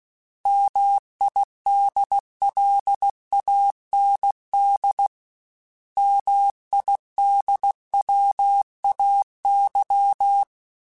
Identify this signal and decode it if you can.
Morse code, heard as a keyed tone, about 16 wpm, 800 Hz. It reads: MIDLAND MIDWAY
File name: fromSMs / FromSMs mp3s / Cities@15